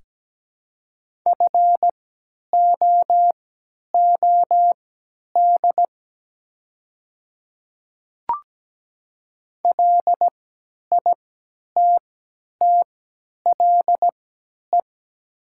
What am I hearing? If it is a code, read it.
Morse code, heard as a keyed tone, about 17 wpm, spaced as for 10 wpm, 700 Hz. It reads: FOOD LITTLE
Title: 3x Character Spacing - Top 300 Words - 17wpm